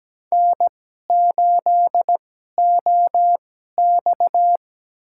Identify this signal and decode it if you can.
Morse code, heard as a keyed tone, about 17 wpm, spaced as for 12 wpm, 700 Hz. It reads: N8OX
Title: USA Call Signs - 4 Characters - 2x Character Spacing - 17wpm